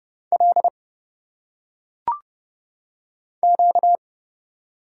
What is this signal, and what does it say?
Morse code, heard as a keyed tone, about 30 wpm, 700 Hz. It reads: L Q
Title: Single Letters - Phonetic Pronunciation 30wpm